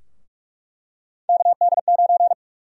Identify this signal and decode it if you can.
Morse code, heard as a keyed tone, about 46 wpm, 700 Hz. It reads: KD9
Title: USA Call Sign Prefixes - 45wpm